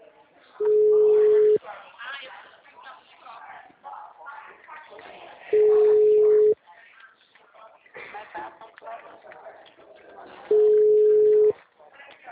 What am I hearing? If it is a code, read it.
Ringback tone (Europe, 425 Hz), with speech.
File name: out- -2017- - - .25943